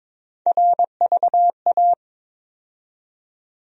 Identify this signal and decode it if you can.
Morse code, heard as a keyed tone, about 22 wpm, 700 Hz. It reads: RVA
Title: USA Call Sign Postfixes - 3 Letters 22wpm